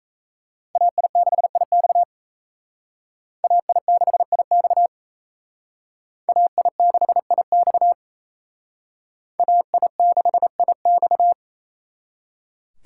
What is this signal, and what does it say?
Morse code, heard as a keyed tone, about 37 wpm, 700 Hz. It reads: AI6IX AI6IX AI6IX AI6IX